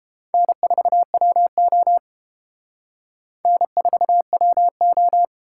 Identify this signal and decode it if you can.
Morse code, heard as a keyed tone, about 32 wpm, 700 Hz. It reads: N4WO N4WO